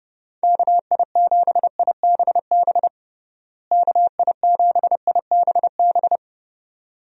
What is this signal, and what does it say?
Morse code, heard as a keyed tone, about 30 wpm, 700 Hz. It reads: KI7IBB KI7IBB
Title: USA Call Signs - 6 Characters - Repeated 2x - 30wpm